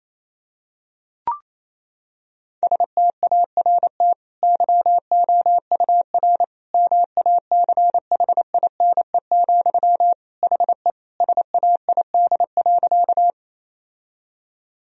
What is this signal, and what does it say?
Morse code, heard as a keyed tone, about 28 wpm, 700 Hz. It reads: START YOUR MACHINE, HE SAID.
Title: Sentences from Top 1000 Words - Review Part 2 - 28wpm